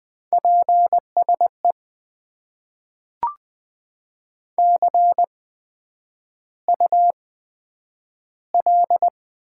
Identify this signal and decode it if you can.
Morse code, heard as a keyed tone, about 20 wpm, spaced as for 9 wpm, 700 Hz. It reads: PSE CUL
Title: Top 90 QSO Elements - 8x Character Spacing - 20wpm